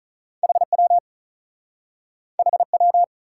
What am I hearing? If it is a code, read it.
Morse code, heard as a keyed tone, about 48 wpm, 700 Hz. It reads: HW HW